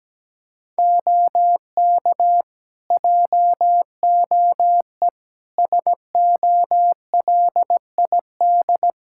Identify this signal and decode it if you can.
Morse code, heard as a keyed tone, about 17 wpm, 700 Hz. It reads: OK JOE SOLID
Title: K1USN SST QSO Practice 17wpm